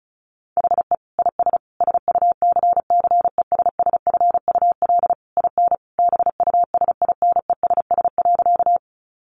Morse code, heard as a keyed tone, about 34 wpm, 700 Hz. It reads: HE IS SUCCESSFUL IN BUSINESS.
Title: Sentences from Top 900 Words 35wpm